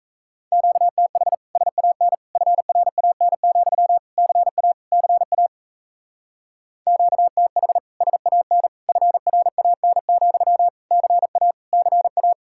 Morse code, heard as a keyed tone, about 34 wpm, 700 Hz. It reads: QTH SAN FRAN, CA CA QTH SAN FRAN, CA CA
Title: Speed-Racing Common QSO Phrases - 28wpm